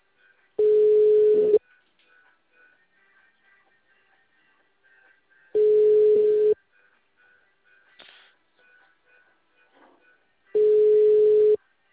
Ringback tone (Europe, 425 Hz).